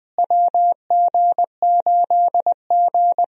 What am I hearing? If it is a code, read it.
Morse code, heard as a keyed tone, about 21 wpm, 700 Hz. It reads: WG8G
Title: USA Call Signs - 4 Characters - Repeated 2x - 20wpm